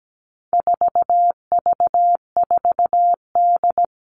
Morse code, heard as a keyed tone, about 17 wpm, 700 Hz. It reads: 4V4D
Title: All Cut Numbers - 4-Digit 17wpm